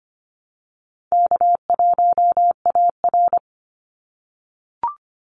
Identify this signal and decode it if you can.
Morse code, heard as a keyed tone, about 25 wpm, 700 Hz. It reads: K1AR
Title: Easy Call Signs - 25wpm